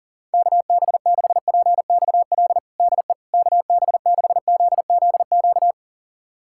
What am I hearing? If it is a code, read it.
Morse code, heard as a keyed tone, about 40 wpm, 700 Hz. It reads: KB6PXL DE KB6ZZQ
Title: USA Callsign DE Callsign - 6 Characters - 40wpm